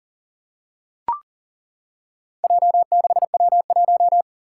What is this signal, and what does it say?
Morse code, heard as a keyed tone, about 40 wpm, 700 Hz. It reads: J6W1
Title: Four Challenging Letter-Number Combinations 40wpm